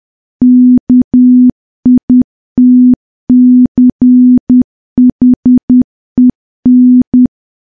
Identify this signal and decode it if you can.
Morse code, heard as a keyed tone, about 10 wpm, 250 Hz. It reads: KITCHEN
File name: morse / morse-training-10WPM